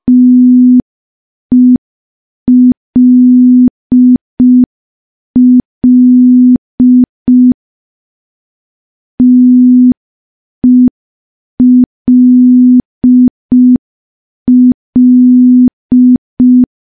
Morse code, heard as a keyed tone, about 5 wpm, 250 Hz. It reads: TELL TELL
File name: morse / morse-training-05WPM